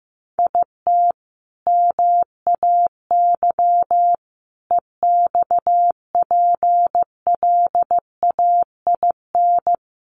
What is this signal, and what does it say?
Morse code, heard as a keyed tone, about 15 wpm, 700 Hz. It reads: IT MAY EXPLAIN
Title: Sentences from Top 1300 Words 15wpm